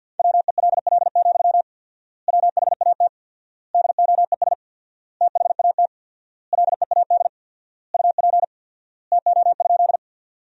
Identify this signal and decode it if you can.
Morse code, heard as a keyed tone, about 52 wpm, 700 Hz. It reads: WELL, WHAT DOES THAT LEAD UP TO?
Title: 4x Word-Spacing - Sentences from Top 500 Words - 50wpm